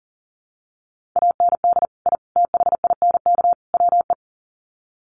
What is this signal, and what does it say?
Morse code, heard as a keyed tone, about 40 wpm, 700 Hz. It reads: AND I THINK WE
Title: Sets of 4 Words 40wpm